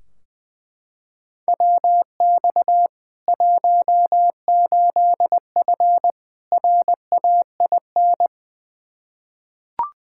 Morse code, heard as a keyed tone, about 20 wpm, 700 Hz. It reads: WX 18F RAIN
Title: Practice QSO - Exchange 2 - Info - 20wpm